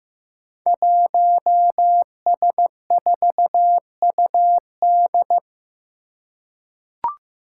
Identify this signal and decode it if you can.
Morse code, heard as a keyed tone, about 15 wpm, 700 Hz. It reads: 1S4UD
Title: Five Challenging Letter-Number Combinations 15wpm